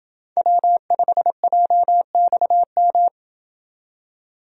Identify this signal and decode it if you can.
Morse code, heard as a keyed tone, about 29 wpm, 700 Hz. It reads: W5JXM